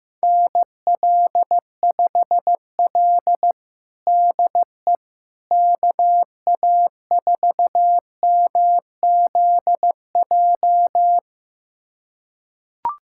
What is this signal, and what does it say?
Morse code, heard as a keyed tone, about 15 wpm, 700 Hz. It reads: NL5L DE KA4MZJ
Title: USA Callsign DE Callsign 15wpm